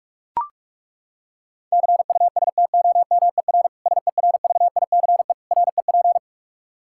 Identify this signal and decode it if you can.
Morse code, heard as a keyed tone, about 41 wpm, 700 Hz. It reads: CUSTOMER SERVICE REP
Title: Practice QSO - Exchange 3 - More Info - 45wpm